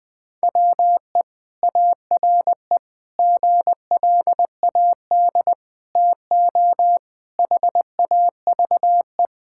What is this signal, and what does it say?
Morse code, heard as a keyed tone, about 19 wpm, 700 Hz. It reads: WE ARE GLAD TO HAVE
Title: Sentences with 5-Word Phrases - Spoken Beginning - 20wpm